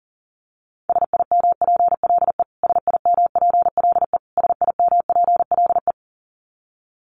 Morse code, heard as a keyed tone, about 40 wpm, 700 Hz. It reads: SIMPLE SIMPLE SIMPLE
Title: Top 500 Words Repeated 40wpm